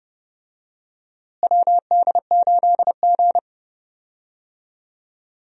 Morse code, heard as a keyed tone, about 32 wpm, 700 Hz. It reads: WD8G
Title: USA Call Signs - 4 Characters - Repeated 2x - 30wpm